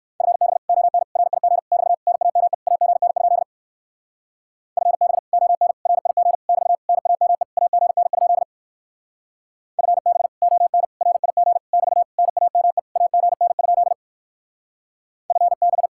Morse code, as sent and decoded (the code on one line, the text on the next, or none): ..-. -... / --- -. / .-. .. --. / -...- / -. .- -- . / .- --. -. ..--.. / ..-. -... / --- -. / .-. .. --. / -...- / -. .- -- . / .- --. -. ..--.. / ..-. -... / --- -. / .-. .. --. / -...- / -. .- -- . / .- --. -. ..--.. / ..-. -...
FB ON RIG <BT> NAME AGN? FB ON RIG <BT> NAME AGN? FB ON RIG <BT> NAME AGN? FB